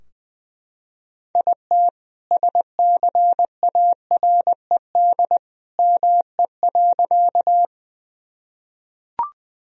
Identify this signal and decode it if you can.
Morse code, heard as a keyed tone, about 21 wpm, 700 Hz. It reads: IT SCARED ME.
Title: Sentences from Top 1000 Words - Review Part 4 - 20wpm